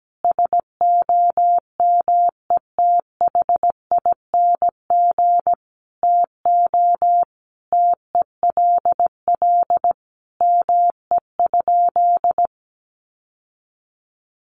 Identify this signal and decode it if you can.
Morse code, heard as a keyed tone, about 17 wpm, 700 Hz. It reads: SOMETHING TO TELL ME?
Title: Sentences from Top 600 Words 17wpm